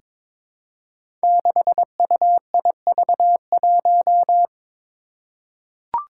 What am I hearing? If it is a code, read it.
Morse code, heard as a keyed tone, about 22 wpm, 700 Hz. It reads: 6UIV1
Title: Code Groups - 5 Letter:Numbers - 22wpm